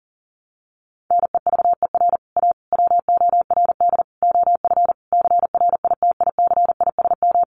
Morse code, heard as a keyed tone, about 40 wpm, 700 Hz. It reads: NEVER A WORD OF CRITICISM
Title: Sentences from Top 1300 Words 40wpm